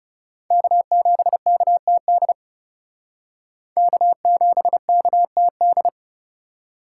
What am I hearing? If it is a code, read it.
Morse code, heard as a keyed tone, about 36 wpm, 700 Hz. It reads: K7KTD K7KTD
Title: Speed Racing - USA 1x3 Call Signs 30wpm